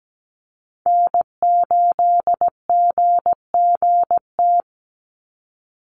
Morse code, heard as a keyed tone, about 17 wpm, 700 Hz. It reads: N8GGT
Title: USA Call Signs 17wpm